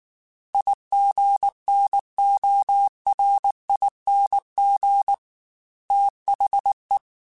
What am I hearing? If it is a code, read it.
Morse code, heard as a keyed tone, about 19 wpm, 800 Hz. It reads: IGNORING THE
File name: Quotes / Quotes-mp3s / IgnoringFacts @18